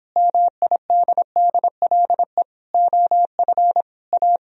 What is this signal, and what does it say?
Morse code, heard as a keyed tone, about 29 wpm, 700 Hz. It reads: MIDDLE OF A